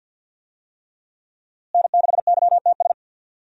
Morse code, heard as a keyed tone, about 52 wpm, 700 Hz. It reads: N6YTS
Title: USA Call Signs - 5 Characters - Repeated 3x - 50wpm